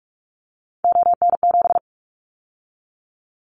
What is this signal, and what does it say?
Morse code, heard as a keyed tone, about 44 wpm, 700 Hz. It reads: ON7
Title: Callsign Prefixes for Top 100 DXCC - 45wpm